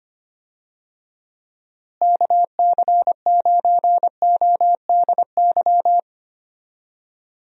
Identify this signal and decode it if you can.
Morse code, heard as a keyed tone, about 25 wpm, 700 Hz. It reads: KC9ODY
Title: USA Call Signs - 6 Characters - Repeated 3x - 25wpm